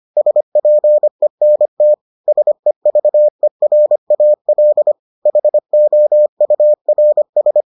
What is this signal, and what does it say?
Morse code, heard as a keyed tone, about 25 wpm, 600 Hz. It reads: SPENT SEVERAL HOURS